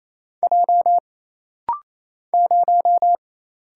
Morse code, heard as a keyed tone, about 28 wpm, 700 Hz. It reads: J 0
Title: Single Letter-Number - Rapid-Fire 28wpm